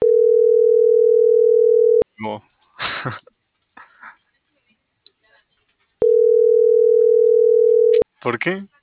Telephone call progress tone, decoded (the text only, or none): ringback tone